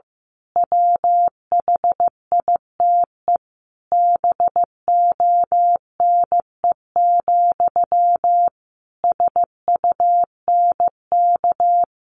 Morse code, heard as a keyed tone, about 15 wpm, 700 Hz. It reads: WHITE BONE, SUNK
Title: A Sound Of Thunder 15wpm